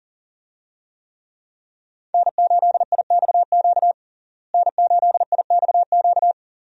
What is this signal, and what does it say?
Morse code, heard as a keyed tone, about 40 wpm, 700 Hz. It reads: N8IXQ N8IXQ
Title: USA Call Signs - 5 Characters - Repeated 2x - 40wpm